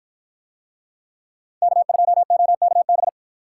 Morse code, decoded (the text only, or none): K2OKB